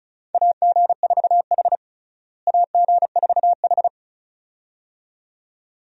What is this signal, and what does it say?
Morse code, heard as a keyed tone, about 34 wpm, 700 Hz. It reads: AG4H AG4H
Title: USA Call Signs - 4 Characters - Repeated 2x - 35wpm